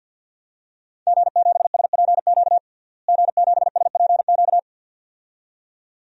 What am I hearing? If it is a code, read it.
Morse code, heard as a keyed tone, about 51 wpm, 700 Hz. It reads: G7SPQ G7SPQ
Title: DXCC-100 Call Signs - 5 Characters - Repeated 2x - 50wpm